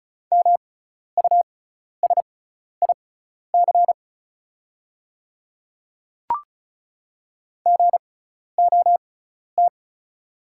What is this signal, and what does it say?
Morse code, heard as a keyed tone, about 34 wpm, spaced as for 12 wpm, 700 Hz. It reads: MUSIC GOT
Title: 6x Character Spacing - Top 300 Words - 35wpm